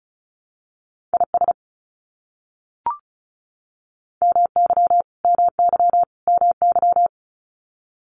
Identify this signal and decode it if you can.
Morse code, heard as a keyed tone, about 34 wpm, 700 Hz. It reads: IS MY MY MY
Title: Top 500 Words Repeated 35wpm